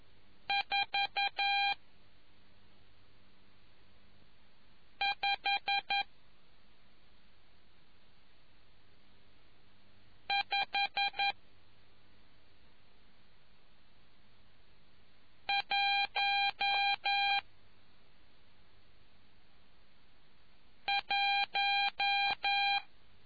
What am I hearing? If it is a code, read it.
Morse code, heard as a keyed tone, about 11 wpm, spaced as for 2 wpm, 800 Hz. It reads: 45511